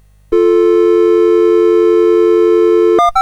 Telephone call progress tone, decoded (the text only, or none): dial tone